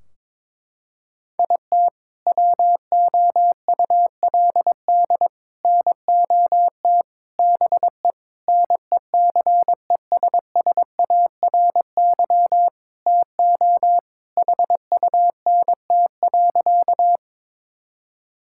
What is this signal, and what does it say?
Morse code, heard as a keyed tone, about 21 wpm, 700 Hz. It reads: IT WOULD NOT BE NECESSARY TO HUNT.
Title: Sentences from Top 1600 Words 22wpm